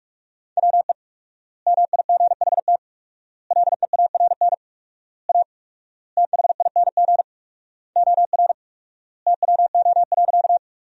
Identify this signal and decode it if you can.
Morse code, heard as a keyed tone, about 46 wpm, 700 Hz. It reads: WE MIGHT LEARN A THING OR TWO.
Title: 4x Word-Spacing - Sentences from Top 300 Words - 45wpm